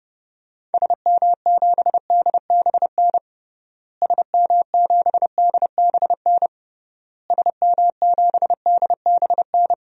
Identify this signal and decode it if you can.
Morse code, heard as a keyed tone, about 30 wpm, 700 Hz. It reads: SM7DBN SM7DBN SM7DBN
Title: DXCC-100 Call Signs - 6 Characters - Repeated 3x - 30wpm